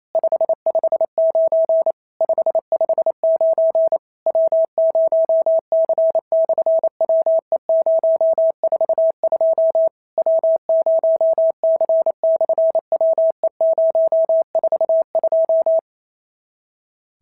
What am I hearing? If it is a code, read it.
Morse code, heard as a keyed tone, about 28 wpm, 650 Hz. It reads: 559 559 W0C/WE042 W0C/WE042
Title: SOTA QSO Practice 28wpm